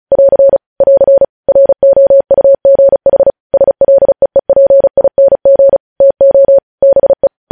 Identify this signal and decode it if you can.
Morse code, heard as a keyed tone, about 36 wpm, 550 Hz. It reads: <AR> <AR> ROUGH SLEEPING TO BE
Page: news headlines in morse code at 35 wpm